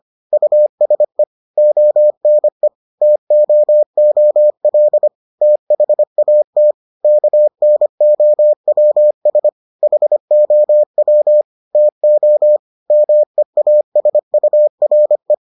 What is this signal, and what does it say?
Morse code, heard as a keyed tone, about 25 wpm, 600 Hz. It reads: USE ONE TOOL THAT KNOWS HOW TO MEASURE